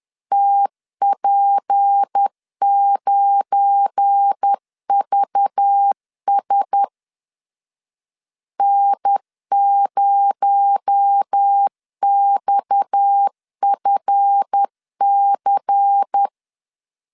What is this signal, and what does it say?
Morse code, heard as a keyed tone, about 10 wpm, 800 Hz. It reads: TP9VS N0XFC
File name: LEZIONI / 7 GF9Z / 7 GF9 10wpm 2747